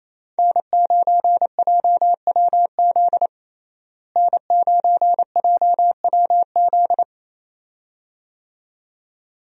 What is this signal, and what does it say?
Morse code, heard as a keyed tone, about 28 wpm, 700 Hz. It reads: N9JWZ N9JWZ
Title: USA Call Signs - 5 Characters - Repeated 2x - 28wpm